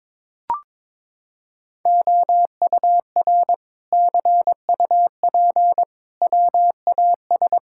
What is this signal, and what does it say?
Morse code, heard as a keyed tone, about 21 wpm, 700 Hz. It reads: OUR CUP WAS
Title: Sentences from Top 1600 Words 22wpm